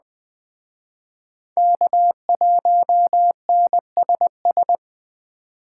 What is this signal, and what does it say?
Morse code, heard as a keyed tone, about 21 wpm, 700 Hz. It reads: K1NSS